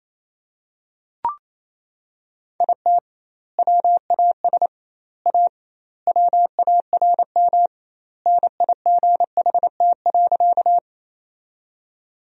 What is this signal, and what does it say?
Morse code, heard as a keyed tone, about 28 wpm, 700 Hz. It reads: IT WAS A WARM NIGHT.
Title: 2x Word-Spacing - Sentences from Top 500 Words - 28wpm